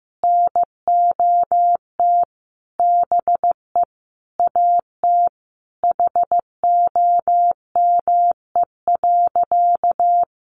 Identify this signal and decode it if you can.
Morse code, heard as a keyed tone, about 15 wpm, 700 Hz. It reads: NOT BE AT HOME.
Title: Sentences from Top 1500 Words 15wpm